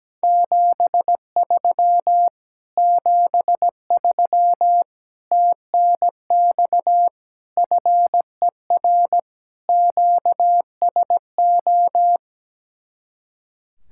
Morse code, heard as a keyed tone, about 17 wpm, 700 Hz. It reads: 73 73 TNX FER QSO
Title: Speed-Racing Common QSO Phrases - 17wpm